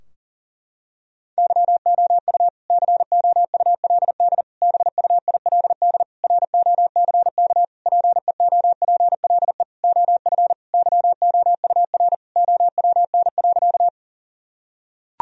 Morse code, heard as a keyed tone, about 40 wpm, 700 Hz. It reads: YOU COULD BUILD ROCK PEOPLE OF YOUR OWN.
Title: Sentences from Top 1000 Words - Review Part 2 - 40wpm